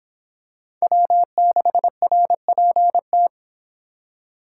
Morse code, heard as a keyed tone, about 25 wpm, 700 Hz. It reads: W6RPT